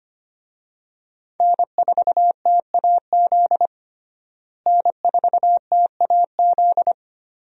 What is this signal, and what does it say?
Morse code, heard as a keyed tone, about 25 wpm, 690 Hz. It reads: N4TAZ N4TAZ
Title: USA Call Signs - 5 Characters - Repeated 2x - 25wpm